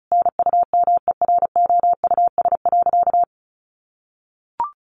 Morse code, heard as a keyed tone, about 34 wpm, 700 Hz. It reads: NUMEROUS.
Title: Sentences from Top 1400 Words 35wpm